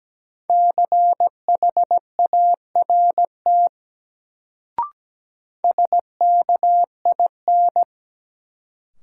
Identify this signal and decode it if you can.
Morse code, heard as a keyed tone, about 17 wpm, 700 Hz. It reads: CHART SKIN
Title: Sentences from Top 1300 Words - New Word Drill - 17wpm